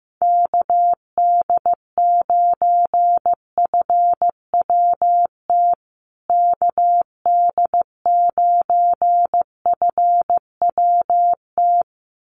Morse code, heard as a keyed tone, about 15 wpm, 700 Hz. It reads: KD9FWT KD9FWT
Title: USA Call Signs Repeated 2x 15wpm